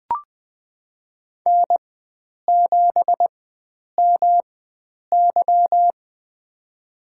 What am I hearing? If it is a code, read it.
Morse code, heard as a keyed tone, about 21 wpm, spaced as for 9 wpm, 700 Hz. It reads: N7MY
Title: USA Call Signs - 4x Character Spacing - 20wpm